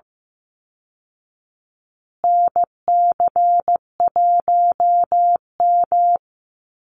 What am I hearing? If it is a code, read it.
Morse code, heard as a keyed tone, about 15 wpm, 700 Hz. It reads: NC1M
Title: Call Signs Repeated 2x - 15wpm